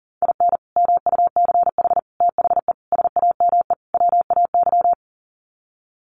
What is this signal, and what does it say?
Morse code, heard as a keyed tone, about 40 wpm, 700 Hz. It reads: IN MUCH THE SAME WAY
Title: Sets of 5 Words Encore 40wpm